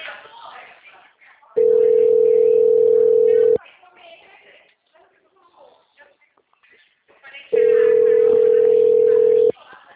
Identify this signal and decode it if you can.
Ringback tone (North America), 440 + 480 Hz, with speech.
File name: out- -2020- - - .317